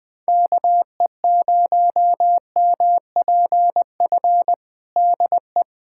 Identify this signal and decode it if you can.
Morse code, heard as a keyed tone, about 20 wpm, 700 Hz. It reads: KE0MPF DE
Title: Practice QSO - Exchange 4 - Ending - 20wpm